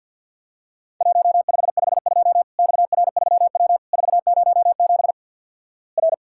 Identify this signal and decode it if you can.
Morse code, heard as a keyed tone, about 50 wpm, 700 Hz. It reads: 1552 XR3W 407 R
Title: Worked All Europe DX Contest - QTC Exchange - 50wpm